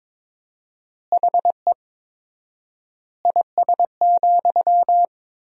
Morse code, heard as a keyed tone, about 22 wpm, 700 Hz. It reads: HE IS,
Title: 4x Word-Spacing - Sentences from Top 100 Words - 22wpm